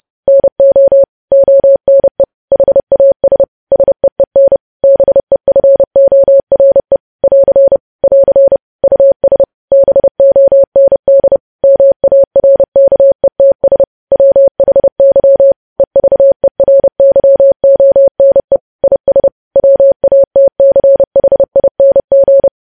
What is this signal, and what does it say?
Morse code, heard as a keyed tone, about 30 wpm, 550 Hz. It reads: NO ONE HAS SEEN BEFORE <AR> <AR> US BOND MARKETS WHY EVERYONE IS WATCHING
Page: news headlines in morse code at 30 wpm